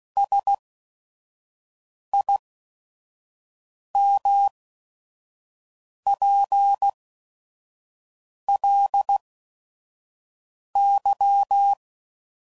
Morse code, heard as a keyed tone, about 16 wpm, spaced as for 5 wpm, 800 Hz. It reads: SIMPLY
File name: CW Code Groups @5WPM / Technical Subjects / XL